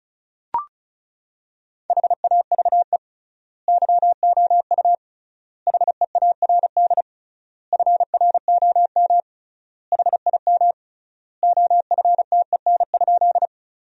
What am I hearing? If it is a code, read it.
Morse code, heard as a keyed tone, about 34 wpm, 700 Hz. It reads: HAVE YOU HEARD FROM HIM OFTEN?
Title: 3x Word-Spacing - Sentences from Top 400 Words - 35wpm